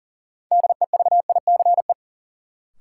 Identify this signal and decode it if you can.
Morse code, heard as a keyed tone, about 40 wpm, 700 Hz. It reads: DEVICE
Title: Sentences from Top 1500 Words - New Word Drill - 40wpm